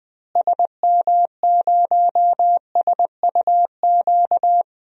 Morse code, heard as a keyed tone, about 20 wpm, 700 Hz. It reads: SM0SUQ
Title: DXCC-100 Call Signs - 6 Characters - Repeated 3x - 20wpm